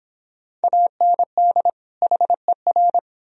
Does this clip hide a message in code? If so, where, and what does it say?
Yes: Morse code from 0.6 to 3.0 s, heard as a keyed tone, about 26 wpm, 700 Hz. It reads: AND HER